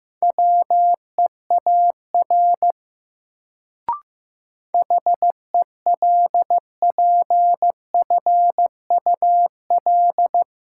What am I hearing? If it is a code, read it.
Morse code, heard as a keyed tone, about 15 wpm, 700 Hz. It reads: WEAR HELPFUL
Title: Sentences from Top 1000 Words - Review Part 6 - New Word Drill - 15wpm